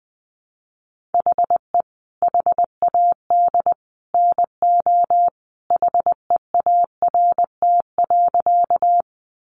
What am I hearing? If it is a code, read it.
Morse code, heard as a keyed tone, about 20 wpm, 700 Hz. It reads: HE HAD NO HEART.
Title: Sentences from Top 1200 Words 20wpm